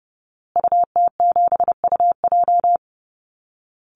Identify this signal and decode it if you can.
Morse code, heard as a keyed tone, about 30 wpm, 700 Hz. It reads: UT7UJ
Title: Call Signs Repeated 2x - 30wpm